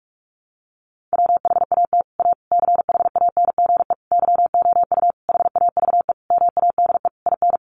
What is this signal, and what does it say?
Morse code, heard as a keyed tone, about 44 wpm, 700 Hz. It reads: WHAT A CHANGE YOU HAVE MADE IN
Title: Sentences from Top 600 Words 45wpm